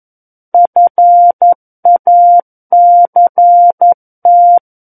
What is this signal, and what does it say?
Morse code, heard as a keyed tone, about 11 wpm, 700 Hz. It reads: FACT
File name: Book One / 02-Chapter 2 - 11 WPM